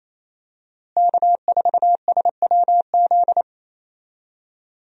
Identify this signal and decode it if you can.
Morse code, heard as a keyed tone, about 28 wpm, 700 Hz. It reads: K4SWZ
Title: USA Call Signs - 5 Characters - Repeated 2x - 28wpm